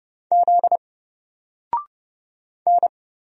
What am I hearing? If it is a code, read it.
Morse code, heard as a keyed tone, about 30 wpm, 700 Hz. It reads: Z N
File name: Single-Letter-Number-30wpm